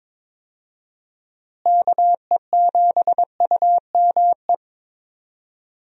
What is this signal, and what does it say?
Morse code, heard as a keyed tone, about 22 wpm, 700 Hz. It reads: KE7UME
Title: USA Call Signs - 6 Characters - Repeated 2x - 22wpm